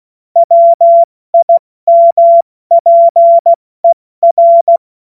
Morse code, heard as a keyed tone, about 16 wpm, spaced as for 14 wpm, 680 Hz. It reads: WIMPER
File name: Morsekurs / Witze / Witz 21.November 2023 70 F80